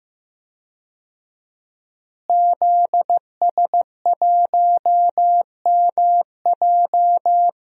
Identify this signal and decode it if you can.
Morse code, heard as a keyed tone, about 15 wpm, 700 Hz. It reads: ZS1MJ
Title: DXCC-100 Call Signs - 5 Characters - Repeated 2x - 15wpm